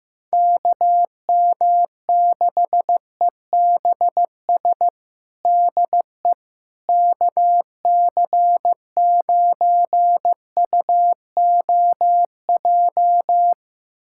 Morse code, heard as a keyed tone, about 15 wpm, 700 Hz. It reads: KM6EBS DE KC9UOJ
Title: USA Callsign DE Callsign - 6 Characters - 15wpm